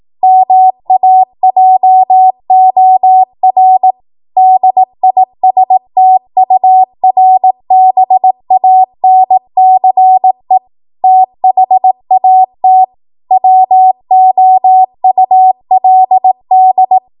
Morse code, heard as a keyed tone, about 18 wpm, 750 Hz. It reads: MAJOR DISTURBANCE THAT WOULD
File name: Morse / 18WPM